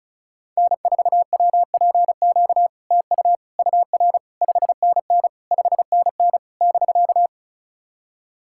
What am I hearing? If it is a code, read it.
Morse code, heard as a keyed tone, about 36 wpm, 700 Hz. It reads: N4WPQ TU UR 5NN 5NN <BK>
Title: SOTA QSO Practice 35wpm